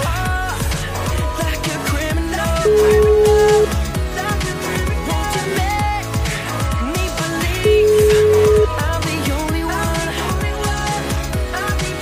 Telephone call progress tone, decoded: ringback tone

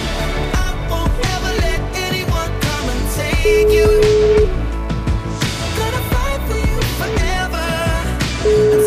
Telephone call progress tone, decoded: ringback tone

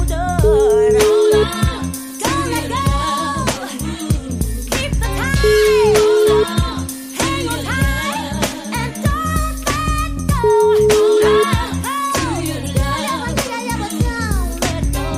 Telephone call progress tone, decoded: ringback tone

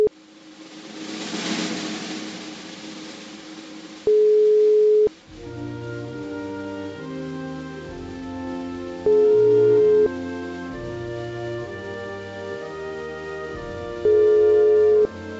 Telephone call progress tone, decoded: ringback tone

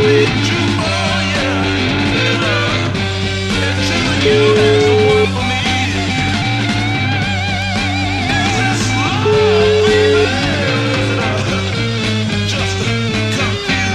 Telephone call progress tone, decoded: ringback tone

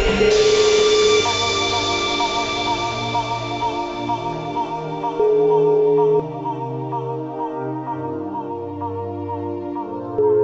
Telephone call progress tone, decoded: ringback tone

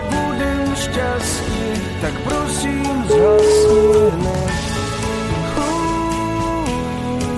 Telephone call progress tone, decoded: ringback tone

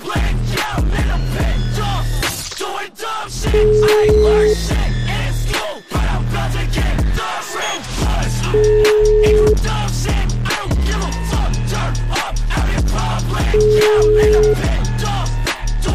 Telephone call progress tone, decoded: ringback tone

